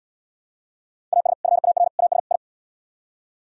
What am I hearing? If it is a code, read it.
Morse code, heard as a keyed tone, about 72 wpm, 700 Hz. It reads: AS FAR AS I